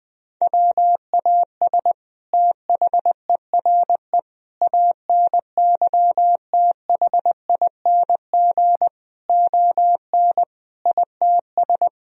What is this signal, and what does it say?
Morse code, heard as a keyed tone, about 20 wpm, 700 Hz. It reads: WAS THERE ANYTHING ON ITS
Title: Sentences from Top 1000 Words - Review Part 3 - 20wpm